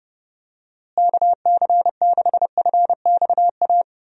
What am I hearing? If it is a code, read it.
Morse code, heard as a keyed tone, about 30 wpm, 700 Hz. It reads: KC6FXA
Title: USA Call Signs - 6 Characters - Repeated 2x - 30wpm